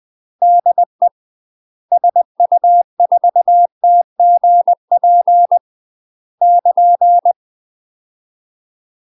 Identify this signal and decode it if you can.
Morse code, heard as a keyed tone, about 20 wpm, 700 Hz. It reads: DE SU4TGP <KN>
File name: UK MASTER QSOs TRAINING / 20WPM / UK MASTER QSOs 0048